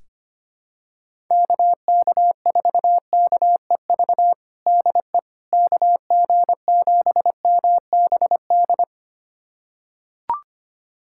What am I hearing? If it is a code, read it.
Morse code, heard as a keyed tone, about 25 wpm, 700 Hz. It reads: KK4KEV DE KG7MBD
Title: USA Callsign DE Callsign - 6 Characters - 25wpm